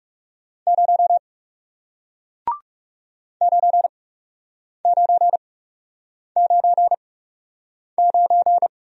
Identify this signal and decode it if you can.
Morse code, heard as a keyed tone, about 40 wpm, spaced as for 8 wpm, 700 Hz. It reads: 0 9999